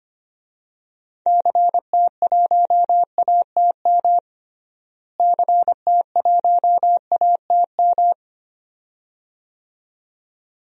Morse code, heard as a keyed tone, about 25 wpm, 700 Hz. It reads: CT1ATM CT1ATM
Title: DXCC-100 Call Signs - 6 Characters - Repeated 2x - 25wpm